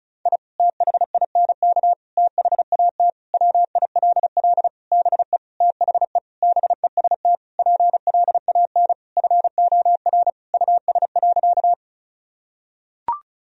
Morse code, heard as a keyed tone, about 34 wpm, 700 Hz. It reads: I THINK THAT WILL BE THE BEST PLAN FOR US.
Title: Sentences from Top 1000 Words - Review Part 2 - 35wpm